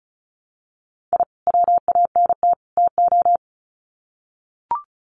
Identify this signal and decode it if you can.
Morse code, heard as a keyed tone, about 34 wpm, 700 Hz. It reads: I WANT TO
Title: Sets of 3 Words 35wpm